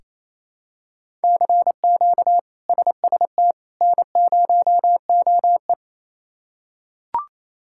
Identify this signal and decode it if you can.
Morse code, heard as a keyed tone, about 28 wpm, 700 Hz. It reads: CQ SST N0OE
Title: K1USN SST QSO Practice - 28wpm